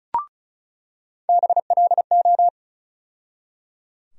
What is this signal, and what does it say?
Morse code, heard as a keyed tone, about 34 wpm, 700 Hz. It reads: BLO
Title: USA Call Sign Postfixes - 3 Letters 35wpm